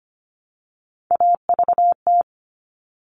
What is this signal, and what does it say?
Morse code, heard as a keyed tone, about 25 wpm, 700 Hz. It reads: AVT
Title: All Cut Numbers - 3-Digit 25wpm